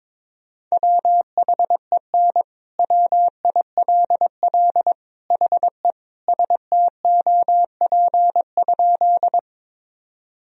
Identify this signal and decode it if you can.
Morse code, heard as a keyed tone, about 22 wpm, 700 Hz. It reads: WHEN WILL HE STOP?
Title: Sentences from Top 1000 Words - Review Part 1 - 22wpm